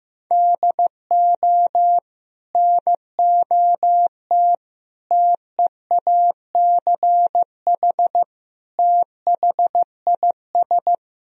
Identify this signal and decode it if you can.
Morse code, heard as a keyed tone, about 15 wpm, 700 Hz. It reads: DO NOT TEACH THIS
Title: Sentences from Top 1000 Words - Review Part 2 - 15wpm